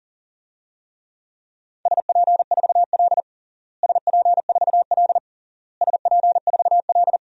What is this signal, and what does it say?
Morse code, heard as a keyed tone, about 40 wpm, 700 Hz. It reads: SP4L SP4L SP4L
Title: DXCC-100 Call Signs - 4 Characters - Repeated 3x - 40wpm